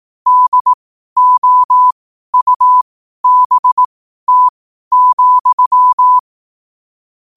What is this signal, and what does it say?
Morse code, heard as a keyed tone, about 18 wpm, spaced as for 12 wpm, 1000 Hz. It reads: DOUBT,
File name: qotdfiles / 12wpm